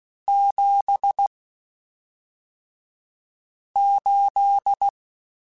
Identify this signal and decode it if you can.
Morse code, heard as a keyed tone, about 16 wpm, spaced as for 3 wpm, 800 Hz. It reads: 78
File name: (spaced) / QSOs Spaced-mp3s / kc7haa @5 sp